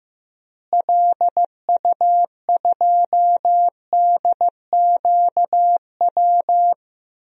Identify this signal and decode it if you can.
Morse code, heard as a keyed tone, about 15 wpm, 700 Hz. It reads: LU2DQW
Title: DXCC-100 Call Signs - 6 Characters - Repeated 3x - 15wpm